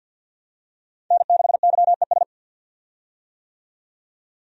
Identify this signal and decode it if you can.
Morse code, heard as a keyed tone, about 50 wpm, 700 Hz. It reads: N6YES
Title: USA Call Signs - 5 Characters - 50wpm